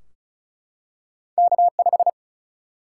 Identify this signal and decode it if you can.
Morse code, heard as a keyed tone, about 36 wpm, 700 Hz. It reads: K 5
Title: Single-Letter Single-Number 35wpm